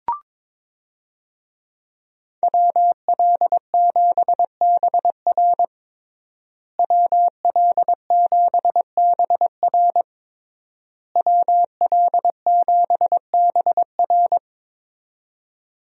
Morse code, heard as a keyed tone, about 21 wpm, 700 Hz. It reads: WL7BR WL7BR WL7BR
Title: USA Call Signs - 5 Characters - Repeated 3x - 22wpm